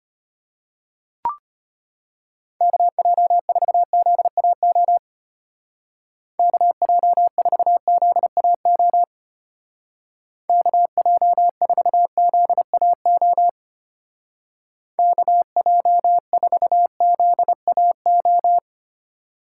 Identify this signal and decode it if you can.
Morse code, heard as a keyed tone, about 30 wpm, 700 Hz. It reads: KJ4ZAO KJ4ZAO KJ4ZAO KJ4ZAO
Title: Speed Racing - USA 2x3 Call Signs 25wpm